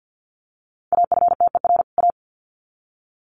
Morse code, heard as a keyed tone, about 50 wpm, 700 Hz. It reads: AFTER A
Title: Sets of 2 Words 50wpm